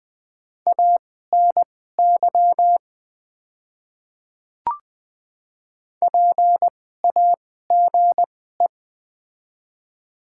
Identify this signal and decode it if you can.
Morse code, heard as a keyed tone, about 20 wpm, spaced as for 15 wpm, 700 Hz. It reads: ANY PAGE
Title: 2x Character Spacing - Top 300 Words - 20wpm